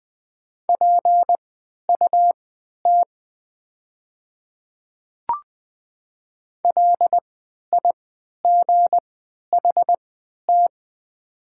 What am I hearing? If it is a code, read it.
Morse code, heard as a keyed tone, about 20 wpm, spaced as for 11 wpm, 700 Hz. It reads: PUT LIGHT
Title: 3x Character Spacing - Top 300 Words - 20wpm